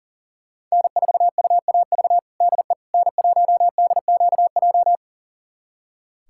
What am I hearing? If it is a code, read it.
Morse code, heard as a keyed tone, about 40 wpm, 700 Hz. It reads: N4UAV DE N1DQJ
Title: USA Callsign DE Callsign - 5 Characters - 40wpm